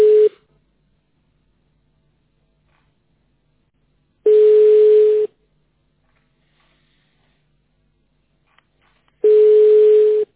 Ringback tone (Europe, 425 Hz).
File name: 2021. .306eabfc-cc2f-41ca-a4be